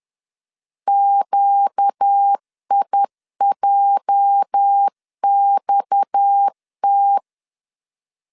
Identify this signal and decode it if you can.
Morse code, heard as a keyed tone, about 10 wpm, 800 Hz. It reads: QIJXT
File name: LEZIONI / 8 YJ45 / 8 YJ 10wpm 2651